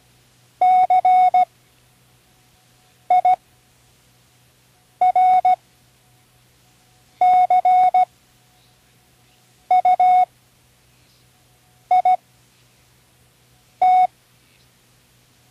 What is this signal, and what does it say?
Morse code, heard as a keyed tone, about 16 wpm, spaced as for 5 wpm, 740 Hz. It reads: CIRCUIT
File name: Learning Morse Code (from Tune in the World)